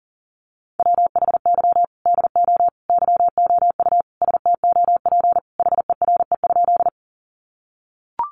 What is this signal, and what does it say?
Morse code, heard as a keyed tone, about 40 wpm, 700 Hz. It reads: WHY DO YOU STOP HERE?
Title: Sentences from Top 300 Words 40wpm